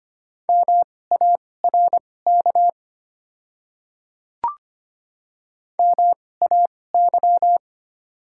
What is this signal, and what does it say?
Morse code, heard as a keyed tone, about 25 wpm, spaced as for 18 wpm, 700 Hz. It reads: MARK MAY
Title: 2x Character Spacing - Top 300 Words - 25wpm